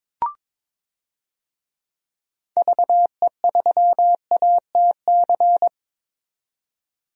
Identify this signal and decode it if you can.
Morse code, heard as a keyed tone, about 22 wpm, 700 Hz. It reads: VE3ATC
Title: DXCC-100 Call Signs - 6 Characters - 22wpm